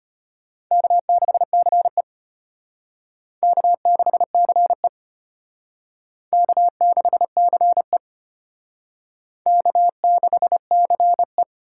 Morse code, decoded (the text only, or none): K6CE K6CE K6CE K6CE